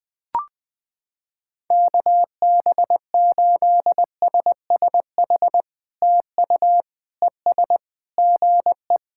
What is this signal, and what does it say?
Morse code, heard as a keyed tone, about 20 wpm, 700 Hz. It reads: KB8SSH TU ES GE
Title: POTA QSO Practice - 20wpm